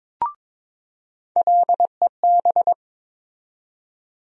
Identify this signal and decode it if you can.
Morse code, heard as a keyed tone, about 22 wpm, 700 Hz. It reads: LEB
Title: USA Call Sign Postfixes - 3 Letters 22wpm